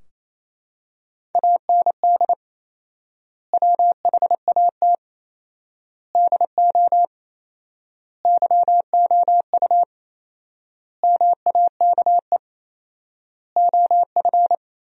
Morse code, heard as a keyed tone, about 27 wpm, 700 Hz. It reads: AND WHAT DO YOU MAKE OF
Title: 4x Word-Spacing - Sentences from Top 100 Words - 28wpm